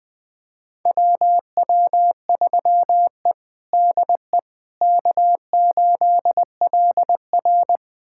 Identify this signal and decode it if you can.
Morse code, heard as a keyed tone, about 20 wpm, 700 Hz. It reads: WW3E DE K8LR
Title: USA Callsign DE Callsign - 4 Characters - 20wpm